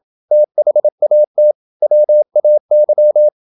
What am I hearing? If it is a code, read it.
Morse code, heard as a keyed tone, about 27 wpm, 600 Hz. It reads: THAT WAY
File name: Never 0027wpm